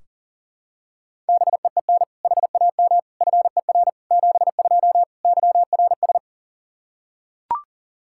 Morse code, heard as a keyed tone, about 40 wpm, 700 Hz. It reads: BEEN HAM FER 72 YRS
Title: Practice QSO - Exchange 3 - More Info - 40wpm